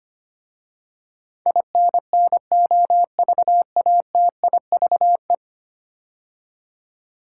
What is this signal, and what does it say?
Morse code, heard as a keyed tone, about 25 wpm, 700 Hz. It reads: INNOVATIVE
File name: 10 Letter Words - First 3 Letters Given - 25wpm